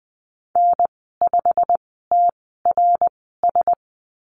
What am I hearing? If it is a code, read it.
Morse code, heard as a keyed tone, about 20 wpm, spaced as for 15 wpm, 700 Hz. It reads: N5TRS
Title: 088 Lesson 15g - Number 5 - Call Signs 20wpm